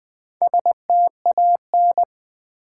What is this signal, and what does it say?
Morse code, heard as a keyed tone, about 20 wpm, 690 Hz. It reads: STAN
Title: Build A Word - 2 Letters At A Time - 6-Letter Words - 20wpm